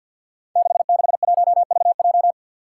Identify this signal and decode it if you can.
Morse code, heard as a keyed tone, about 47 wpm, 700 Hz. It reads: 661VJ
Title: Five Challenging Letter-Number Combinations 50wpm